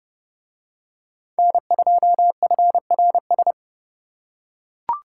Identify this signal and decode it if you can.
Morse code, heard as a keyed tone, about 29 wpm, 700 Hz. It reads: N2FRS